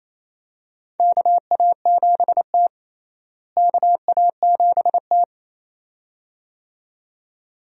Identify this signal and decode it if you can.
Morse code, heard as a keyed tone, about 28 wpm, 700 Hz. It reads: KA7T KA7T
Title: USA Call Signs - 4 Characters - Repeated 2x - 28wpm